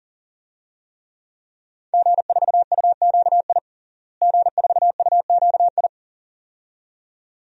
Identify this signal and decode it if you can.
Morse code, heard as a keyed tone, about 40 wpm, 700 Hz. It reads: G4UQI G4UQI
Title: DXCC-100 Call Signs - 5 Characters - Repeated 2x - 40wpm